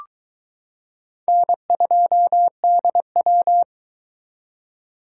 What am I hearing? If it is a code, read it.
Morse code, heard as a keyed tone, about 22 wpm, 700 Hz. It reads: N2DW